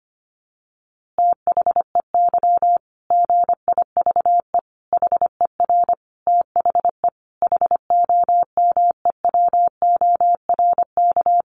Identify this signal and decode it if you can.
Morse code, heard as a keyed tone, about 25 wpm, 700 Hz. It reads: THEY GIVE HER THE HOMEWORK
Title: Easy Phrases 25wpm